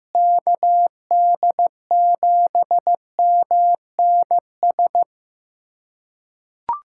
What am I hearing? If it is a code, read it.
Morse code, heard as a keyed tone, about 15 wpm, 700 Hz. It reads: KD7MNS
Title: USA Callsign DE Callsign 15wpm